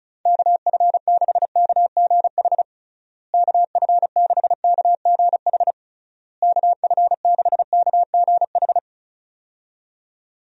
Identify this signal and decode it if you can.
Morse code, heard as a keyed tone, about 34 wpm, 700 Hz. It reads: KF6KGH KF6KGH KF6KGH
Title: USA Call Signs - 6 Characters - Repeated 3x - 35wpm